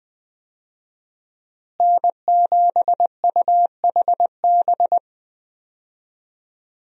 Morse code, heard as a keyed tone, about 20 wpm, 700 Hz. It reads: N7UHB
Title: USA Call Signs - 5 Characters - Repeated 3x - 20wpm